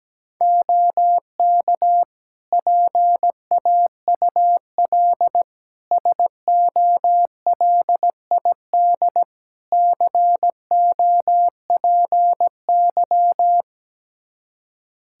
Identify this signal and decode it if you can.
Morse code, heard as a keyed tone, about 17 wpm, 700 Hz. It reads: OK PAUL SOLID COPY
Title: K1USN SST QSO Practice 17wpm